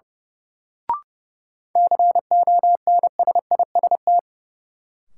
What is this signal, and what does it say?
Morse code, heard as a keyed tone, about 31 wpm, 700 Hz. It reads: CONSIST
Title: Sentences from Top 1400 Words - New Word Drill - 30wpm